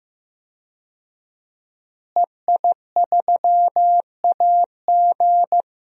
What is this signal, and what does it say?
Morse code, heard as a keyed tone, about 15 wpm, 700 Hz. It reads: EI3AG
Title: DXCC-100 Call Signs - 5 Characters - Repeated 3x - 15wpm